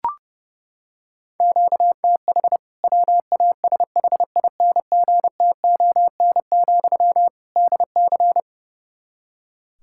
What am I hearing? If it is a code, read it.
Morse code, heard as a keyed tone, about 30 wpm, 700 Hz. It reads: QTH WASHINGTON, DC
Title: K1USN SST QSO Practice 30wpm